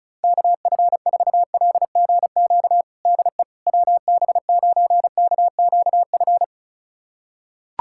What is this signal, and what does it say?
Morse code, heard as a keyed tone, about 34 wpm, 700 Hz. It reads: KF4LGQ DE WB9KQF
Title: USA Callsign DE Callsign - 6 Characters - 35wpm